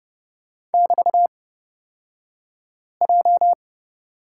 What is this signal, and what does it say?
Morse code, heard as a keyed tone, about 30 wpm, spaced as for 5 wpm, 700 Hz. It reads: <BT>J
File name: Lesson 36d - BT - Two Characters 30wpm